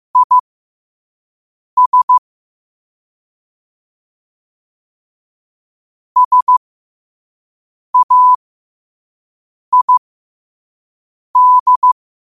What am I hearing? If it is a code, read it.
Morse code, heard as a keyed tone, about 15 wpm, spaced as for 5 wpm, 1000 Hz. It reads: IS SAID
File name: qotdfiles / 5wpm